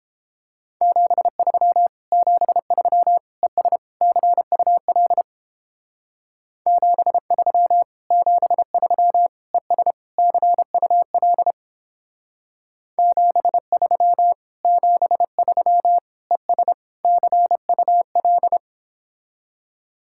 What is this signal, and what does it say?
Morse code, heard as a keyed tone, about 29 wpm, 700 Hz. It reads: 73 73 ES CUL 73 73 ES CUL 73 73 ES CUL